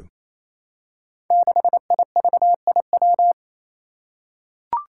Morse code, heard as a keyed tone, about 29 wpm, 700 Hz. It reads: 6IVIW
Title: Five Challenging Letter-Number Combinations 28wpm